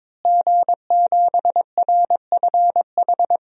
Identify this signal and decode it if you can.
Morse code, heard as a keyed tone, about 21 wpm, 700 Hz. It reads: G7RFH
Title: DXCC-100 Call Signs - 5 Characters - Repeated 2x - 22wpm